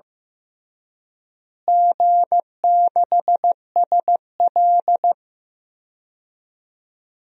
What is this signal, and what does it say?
Morse code, heard as a keyed tone, about 15 wpm, 700 Hz. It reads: G6SL
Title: DXCC-100 Call Signs - 4 Characters - Repeated 2x - 15wpm